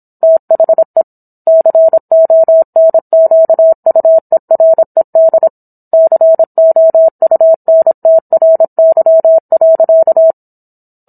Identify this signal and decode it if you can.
Morse code, heard as a keyed tone, about 27 wpm, 650 Hz. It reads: THE CONQUERED COUNTRY.